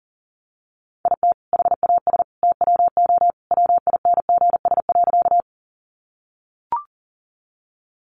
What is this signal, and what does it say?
Morse code, heard as a keyed tone, about 40 wpm, 700 Hz. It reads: IT HAS TWO WINGS.
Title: Additional Easy Phrases 40wpm